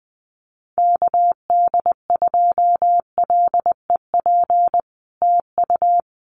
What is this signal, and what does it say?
Morse code, heard as a keyed tone, about 20 wpm, 700 Hz. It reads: KD2LEP TU
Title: Common QSO Phrases Encore 20wpm